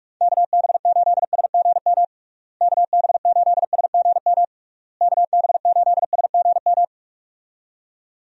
Morse code, heard as a keyed tone, about 41 wpm, 700 Hz. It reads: KB8SGM KB8SGM KB8SGM
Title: USA Call Signs - 6 Characters - Repeated 3x - 45wpm